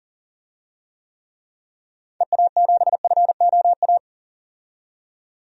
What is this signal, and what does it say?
Morse code, heard as a keyed tone, about 40 wpm, 700 Hz. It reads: EA7FOA
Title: DXCC-100 Call Signs - 6 Characters - 40wpm